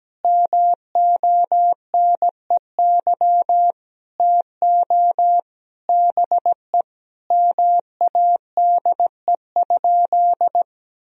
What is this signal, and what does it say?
Morse code, heard as a keyed tone, about 17 wpm, 700 Hz. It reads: MONEY TO BE MADE?
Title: Sentences from Top 1000 Words - Review Part 4 - 17wpm